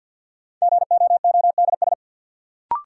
Morse code, heard as a keyed tone, about 51 wpm, 700 Hz. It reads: GOODS
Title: Sentences from Top 1000 Words - Review Part 6 - New Word Drill - 50wpm